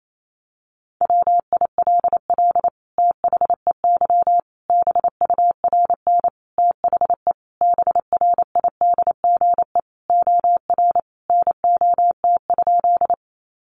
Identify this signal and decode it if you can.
Morse code, heard as a keyed tone, about 28 wpm, 700 Hz. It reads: WILL THEY BURN THE BRIDGE OR NOT?
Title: Sentences from Top 1300 Words 28wpm